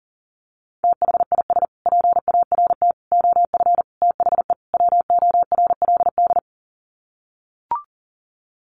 Morse code, heard as a keyed tone, about 40 wpm, 700 Hz. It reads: THIS PART OF THE WORLD
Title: Sets of 5 Words Encore2 40wpm